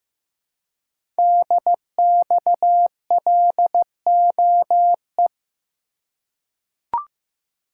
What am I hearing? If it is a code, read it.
Morse code, heard as a keyed tone, about 15 wpm, 700 Hz. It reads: DXLOE
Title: Code Groups - 5 Letters - 15wpm